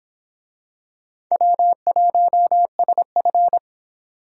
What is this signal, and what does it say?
Morse code, heard as a keyed tone, about 25 wpm, 700 Hz. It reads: W1SF